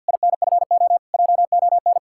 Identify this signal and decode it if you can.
Morse code, heard as a keyed tone, about 59 wpm, 700 Hz. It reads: INFO JON
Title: Speed-Racing Common QSO Phrases - 50wpm